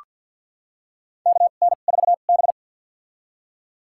Morse code, heard as a keyed tone, about 47 wpm, spaced as for 35 wpm, 700 Hz. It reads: KN4B
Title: USA Call Signs - 4 Characters - 2x Character Spacing - 50wpm